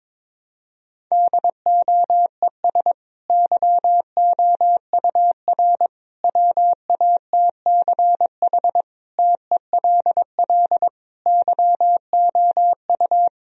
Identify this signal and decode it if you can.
Morse code, heard as a keyed tone, about 22 wpm, 700 Hz. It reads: DOES YOUR WATCH TELL YOU
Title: Sentences from Top 1000 Words - Review Part 1 - 22wpm